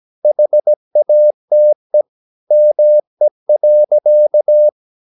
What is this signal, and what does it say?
Morse code, heard as a keyed tone, about 17 wpm, 600 Hz. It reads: HATE ME.
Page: Romanen Never af Ken Follet på engelsk.